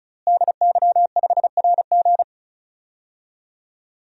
Morse code, heard as a keyed tone, about 36 wpm, 700 Hz. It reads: DY5RG
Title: DXCC-100 Call Signs - 5 Characters - Repeated 2x - 35wpm